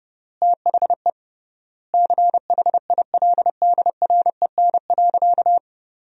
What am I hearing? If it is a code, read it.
Morse code, heard as a keyed tone, about 32 wpm, 700 Hz. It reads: THE CHILDREN.
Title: 3x Word-Spacing - Sentences from Top 300 Words - 30wpm